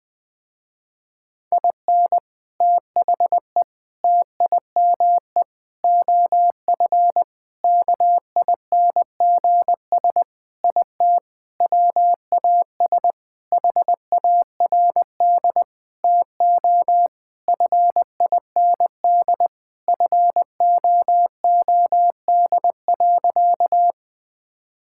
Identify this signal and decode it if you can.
Morse code, heard as a keyed tone, about 20 wpm, 700 Hz. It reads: IN THE TIME OF KINGS IT WAS HARD TO FIND FOOD.
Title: Sentences with 3-Word Phrases - Spoken Beginning - 20wpm